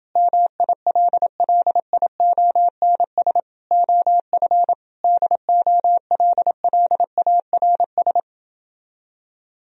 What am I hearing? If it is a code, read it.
Morse code, heard as a keyed tone, about 30 wpm, 700 Hz. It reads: MILLIONS OF DOLLARS